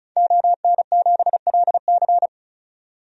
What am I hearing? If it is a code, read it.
Morse code, heard as a keyed tone, about 34 wpm, 700 Hz. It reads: ON7LC
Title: DXCC-100 Call Signs - 5 Characters - 35wpm